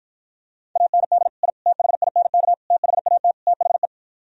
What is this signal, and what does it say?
Morse code, heard as a keyed tone, about 52 wpm, 700 Hz. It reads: AND I THINK THAT THE